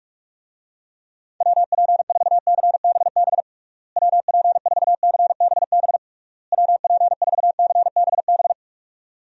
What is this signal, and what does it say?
Morse code, heard as a keyed tone, about 46 wpm, 700 Hz. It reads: WP4CBB WP4CBB WP4CBB
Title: USA Call Signs - 6 Characters - Repeated 3x - 45wpm